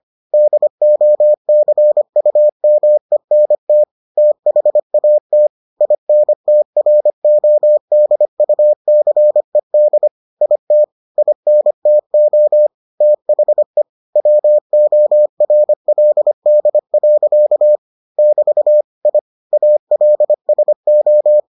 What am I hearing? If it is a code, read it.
Morse code, heard as a keyed tone, about 25 wpm, 600 Hz. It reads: DOCUMENT THAT INTRODUCED IT INTO THE WORLD. <BT> I ALSO